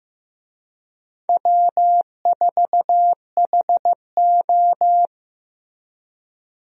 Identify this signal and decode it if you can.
Morse code, heard as a keyed tone, about 15 wpm, 700 Hz. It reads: W4HO
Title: USA Call Signs - 4 Characters - Repeated 2x - 15wpm